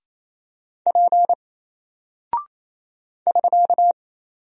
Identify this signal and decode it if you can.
Morse code, heard as a keyed tone, about 27 wpm, 700 Hz. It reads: P <SK>
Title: Single Character (Letter:Number:Punctuation:Prosign) 28wpm